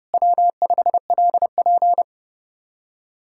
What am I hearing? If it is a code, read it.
Morse code, heard as a keyed tone, about 37 wpm, 700 Hz. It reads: W5LP
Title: Speed-Racing Common QSO Phrases Encore - 30wpm